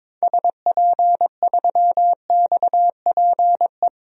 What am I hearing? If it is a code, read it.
Morse code, heard as a keyed tone, about 22 wpm, 700 Hz. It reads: SP3XPE
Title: DXCC-100 Call Signs - 6 Characters - Repeated 2x - 22wpm